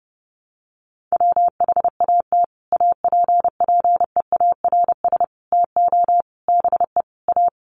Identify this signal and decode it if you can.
Morse code, heard as a keyed tone, about 30 wpm, 700 Hz. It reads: WHAT APPEARS TO BE A
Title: Sets of 5 Words Encore2 30wpm